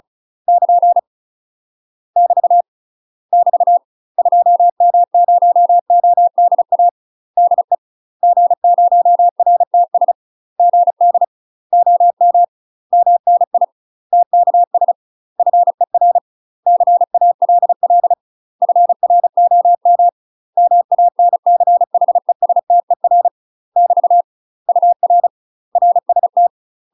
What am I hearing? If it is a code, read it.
Morse code, heard as a keyed tone, about 36 wpm, 700 Hz. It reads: <KN> <BT> <BT> 2M0ODA DE G0RTS GD OM MNI TKS FER CALL FROM MANCHESTER <BT> UR RST
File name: UK MASTER QSOs TRAINING / 35WPM / UK MASTER QSOs 0002